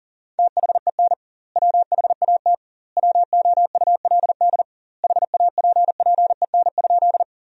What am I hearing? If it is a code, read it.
Morse code, heard as a keyed tone, about 40 wpm, 700 Hz. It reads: THEN WHAT WOULD HAPPEN?
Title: 2x Word-Spacing - Sentences from Top 400 Words - 40wpm